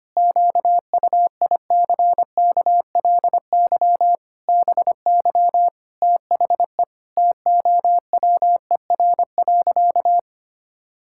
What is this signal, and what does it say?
Morse code, heard as a keyed tone, about 25 wpm, 700 Hz. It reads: QUICKLY BY THE TOWER.
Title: Pangram Challenge v2 - 25wpm